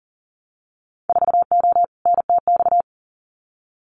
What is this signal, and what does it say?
Morse code, heard as a keyed tone, about 39 wpm, 700 Hz. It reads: 4O NTX
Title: Winter Field Day Exchanges 40wpm